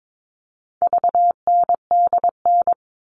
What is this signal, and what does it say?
Morse code, heard as a keyed tone, about 22 wpm, 700 Hz. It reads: VNDN
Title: All Cut Numbers - 4-Digit 22wpm